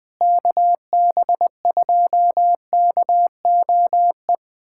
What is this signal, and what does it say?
Morse code, heard as a keyed tone, about 20 wpm, 700 Hz. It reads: KB2KOE
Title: USA Callsign DE Callsign - 6 Characters - 20wpm